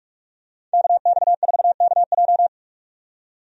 Morse code, heard as a keyed tone, about 46 wpm, 700 Hz. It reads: KX4KJ
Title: USA Call Signs - 5 Characters - 45wpm